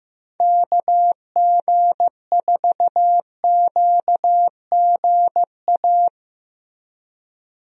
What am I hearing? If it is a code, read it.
Morse code, heard as a keyed tone, about 15 wpm, 700 Hz. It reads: KG4QGA
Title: USA Call Signs - 6 Characters - Repeated 3x - 15wpm